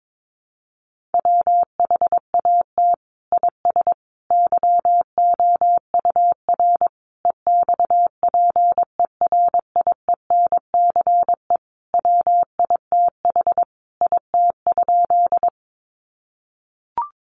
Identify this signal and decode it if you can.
Morse code, heard as a keyed tone, about 22 wpm, 700 Hz. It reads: WHAT IS YOUR EXPERIENCE WITH IT?
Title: Sentences from Top 600 Words 22wpm